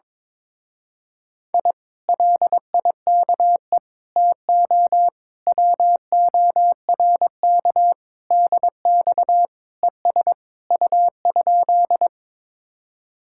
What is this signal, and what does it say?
Morse code, heard as a keyed tone, about 22 wpm, 700 Hz. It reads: I LIKE TO WORK DX ES U?
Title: Speed-Racing Common QSO Phrases Encore - 22wpm